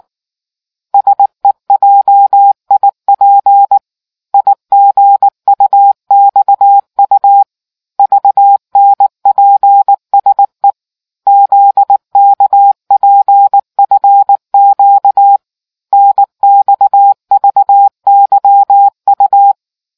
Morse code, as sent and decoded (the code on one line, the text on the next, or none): ... .--- .. .--. / .. --. ..- -..- ..- / ...- -. .--. ... / --.. -.- .--. ..-. --.- / -. -..- ...- -.-- ..-
SEJIP IGUXU VNPSE ZKPFQ NXVYU